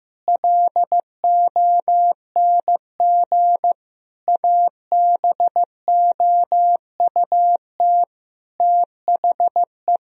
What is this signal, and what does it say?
Morse code, heard as a keyed tone, about 16 wpm, 700 Hz. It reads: LONG ABOUT THE
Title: Sentences from Top 1000 Words - Review Part 3 - 15wpm